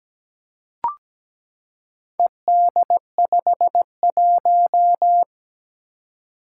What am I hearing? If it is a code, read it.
Morse code, heard as a keyed tone, about 17 wpm, 700 Hz. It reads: ED51
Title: Four Challenging Letter-Number Combinations 17wpm